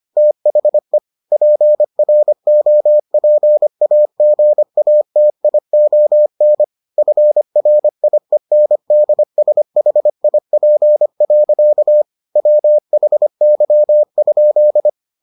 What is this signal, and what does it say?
Morse code, heard as a keyed tone, about 25 wpm, 600 Hz. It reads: THE PROPAGATION FRIENDSHIP. WHY?